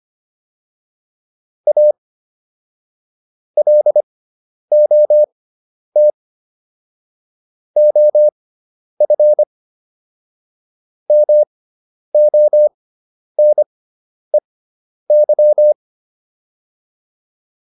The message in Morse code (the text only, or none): A LOT OF MONEY